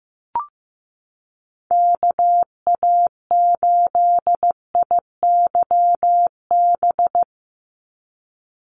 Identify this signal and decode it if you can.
Morse code, heard as a keyed tone, about 15 wpm, 700 Hz. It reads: KA8IYB
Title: USA Call Signs 15wpm